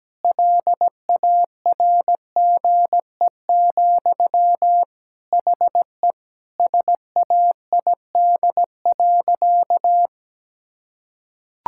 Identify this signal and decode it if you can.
Morse code, heard as a keyed tone, about 17 wpm, 700 Hz. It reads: LARGE, HE SAID.
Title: Sentences from Top 1600 Words 17wpm